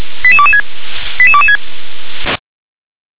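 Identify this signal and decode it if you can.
Five-tone selective call (ZVEI), heard as quick runs of five tones, tones of about 70 ms. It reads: 88226 88226
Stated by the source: ZVEI five-tone selective call 88158, received by SDR.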